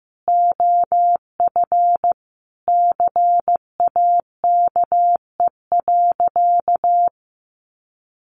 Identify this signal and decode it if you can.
Morse code, heard as a keyed tone, about 15 wpm, 700 Hz. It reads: OF CAKE.
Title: Additional Easy Phrases 15wpm